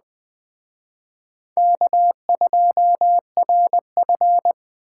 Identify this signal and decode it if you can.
Morse code, heard as a keyed tone, about 19 wpm, 700 Hz. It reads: K2RF